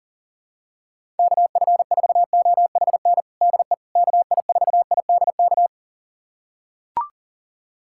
Morse code, heard as a keyed tone, about 40 wpm, 700 Hz. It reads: KF4OHN DE KI4IDK
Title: USA Callsign DE Callsign - 6 Characters - 40wpm